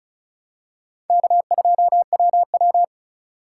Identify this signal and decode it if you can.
Morse code, heard as a keyed tone, about 36 wpm, 700 Hz. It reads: K2WW
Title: USA Call Signs - 2x Character Spacing - 35wpm